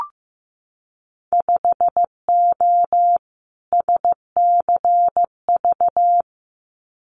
Morse code, heard as a keyed tone, about 15 wpm, 700 Hz. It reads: 5O SCV
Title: Winter Field Day Exchanges Encore 15wpm